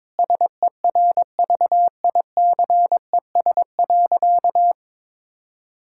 Morse code, heard as a keyed tone, about 22 wpm, 700 Hz. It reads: SERVICES.
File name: practice / 22wpm / Sentences with 4-Word Phrases - 22wpm